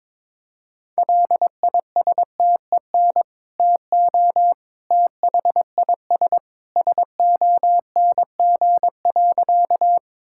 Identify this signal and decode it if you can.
Morse code, heard as a keyed tone, about 22 wpm, 700 Hz. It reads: LISTEN TO THIS SONG.
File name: practice / 22wpm / Sentences with 2-Word Phrases - 22wpm